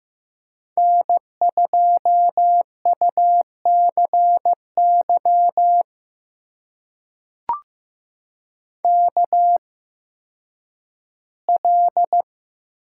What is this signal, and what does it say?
Morse code, heard as a keyed tone, about 15 wpm, spaced as for 10 wpm, 700 Hz. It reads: N2UCY KL
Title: USA Call Signs - 5 Characters - 8x Character Spacing - 15wpm